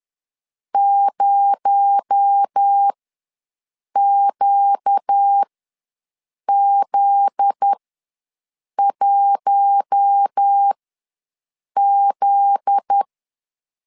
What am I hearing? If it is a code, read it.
Morse code, heard as a keyed tone, about 10 wpm, spaced as for 6 wpm, 800 Hz. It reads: 0QZ1Z
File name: LEZIONI / 7 GF9Z / 7 GF9Z ws 10wpm 3919